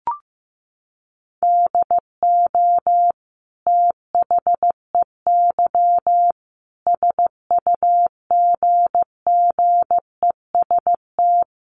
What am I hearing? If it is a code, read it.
Morse code, heard as a keyed tone, about 15 wpm, 700 Hz. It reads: DO THEY SUGGEST
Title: Sentences from Top 1300 Words 15wpm